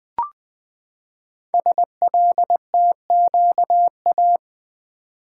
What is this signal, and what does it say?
Morse code, heard as a keyed tone, about 20 wpm, 700 Hz. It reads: SLTQA
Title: Code Groups - 5 Letters - 20wpm